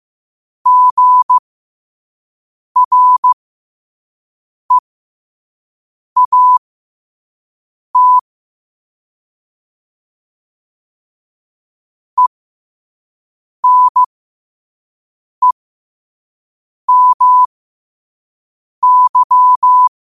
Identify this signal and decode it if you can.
Morse code, heard as a keyed tone, about 15 wpm, spaced as for 5 wpm, 1000 Hz. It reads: GREAT ENEMY
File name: qotdfiles / 5wpm